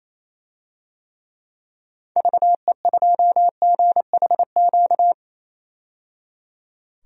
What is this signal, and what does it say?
Morse code, heard as a keyed tone, about 28 wpm, 700 Hz. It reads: VE2GHQ
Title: DXCC-100 Call Signs - 6 Characters - 28wpm